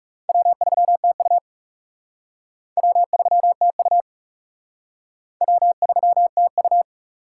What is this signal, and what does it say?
Morse code, heard as a keyed tone, about 38 wpm, 700 Hz. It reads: W3TU W3TU W3TU